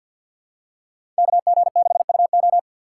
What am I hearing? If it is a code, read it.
Morse code, heard as a keyed tone, about 53 wpm, 700 Hz. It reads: KG6UO
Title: USA Call Signs - 5 Characters - 8x Character Spacing - 50wpm